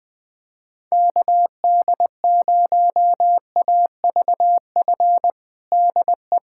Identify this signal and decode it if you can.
Morse code, heard as a keyed tone, about 20 wpm, 700 Hz. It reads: KD0AVF DE
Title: USA Callsign DE Callsign 20wpm